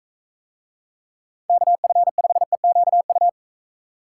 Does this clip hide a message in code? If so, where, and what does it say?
Yes: Morse code from 1.5 to 3.3 s, heard as a keyed tone, about 42 wpm, 700 Hz. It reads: KF5EQU